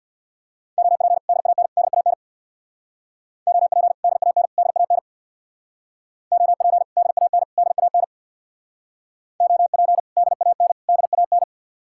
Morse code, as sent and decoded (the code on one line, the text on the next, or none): --- .--. / -.. .- -. / -.. .- -. / --- .--. / -.. .- -. / -.. .- -. / --- .--. / -.. .- -. / -.. .- -. / --- .--. / -.. .- -. / -.. .- -.
OP DAN DAN OP DAN DAN OP DAN DAN OP DAN DAN